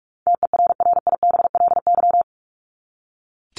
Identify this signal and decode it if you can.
Morse code, heard as a keyed tone, about 44 wpm, 700 Hz. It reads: TERRIBLY
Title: Top 1751-2000 Words 45wpm